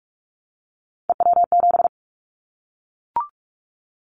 Morse code, heard as a keyed tone, about 46 wpm, 700 Hz. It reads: EW7
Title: Callsign Prefixes for Top 100 DXCC - 45wpm